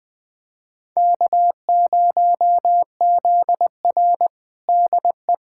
Morse code, heard as a keyed tone, about 20 wpm, 700 Hz. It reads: K0ZR DE
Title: Practice QSO - Exchange 4 - Ending - 20wpm